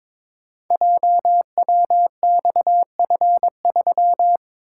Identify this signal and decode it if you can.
Morse code, heard as a keyed tone, about 22 wpm, 700 Hz. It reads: JWXF3
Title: Code Groups - 5 Letter:Numbers - 22wpm